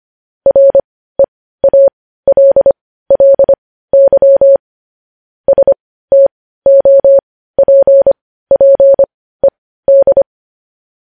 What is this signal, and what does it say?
Morse code, heard as a keyed tone, about 25 wpm, spaced as for 15 wpm, 550 Hz. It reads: REALLY STOPPED
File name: podcast / News headlines-1x-25x15